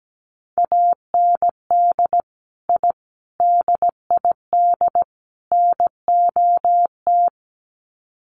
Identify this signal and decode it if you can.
Morse code, heard as a keyed tone, about 17 wpm, 700 Hz. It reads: AND I DID NOT
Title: Sets of 4 Words 17wpm